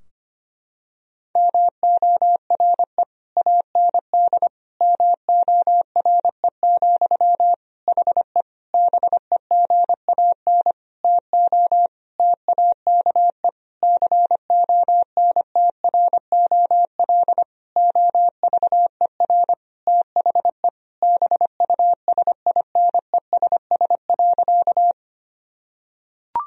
Morse code, heard as a keyed tone, about 25 wpm, 700 Hz. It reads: MORE AND MORE, HE BEGAN TO TAKE CONTROL OVER THE BUSINESS.
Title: Sentences from Top 1000 Words - Review Part 3 - 25wpm